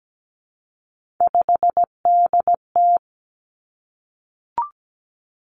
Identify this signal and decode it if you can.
Morse code, heard as a keyed tone, about 17 wpm, 700 Hz. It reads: 5DT
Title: Three Challenging Letter-Number Combinations 17wpm